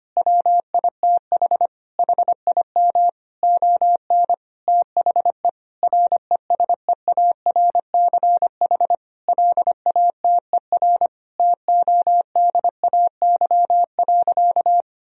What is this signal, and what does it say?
Morse code, heard as a keyed tone, about 25 wpm, 700 Hz. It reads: WITH HIM ON THE RESEARCH LATER TODAY.
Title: Sentences from Top 1000 Words - Review Part 4 - 25wpm